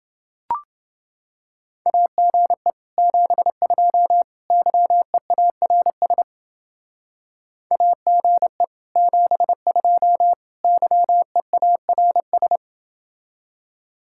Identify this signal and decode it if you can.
Morse code, heard as a keyed tone, about 30 wpm, 700 Hz. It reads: AGE 72 YEARS AGE 72 YEARS